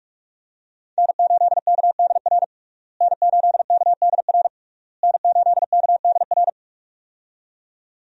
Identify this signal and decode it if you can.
Morse code, heard as a keyed tone, about 41 wpm, 700 Hz. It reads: N8KDR N8KDR N8KDR
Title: USA Call Signs - 5 Characters - Repeated 3x - 45wpm